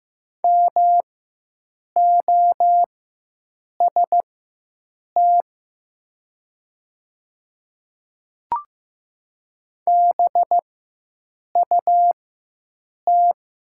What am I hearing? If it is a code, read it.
Morse code, heard as a keyed tone, about 15 wpm, spaced as for 7 wpm, 700 Hz. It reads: MOST BUT
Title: 4x Character Spacing - Top 200 Words - 15wpm